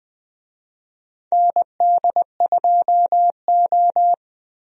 Morse code, heard as a keyed tone, about 21 wpm, 700 Hz. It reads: ND2O
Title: USA Call Signs - 4 Characters - Repeated 2x - 20wpm